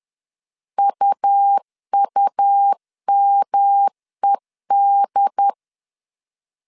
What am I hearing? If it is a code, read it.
Morse code, heard as a keyed tone, about 10 wpm, 800 Hz. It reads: UUMED
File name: CW / DET / 1 3 1500C S C 10wpm 3335